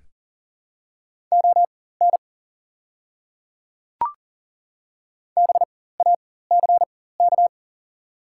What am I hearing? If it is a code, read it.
Morse code, heard as a keyed tone, about 37 wpm, spaced as for 18 wpm, 700 Hz. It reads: ON BACK
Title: 4x Character Spacing - Top 300 Words - 40wpm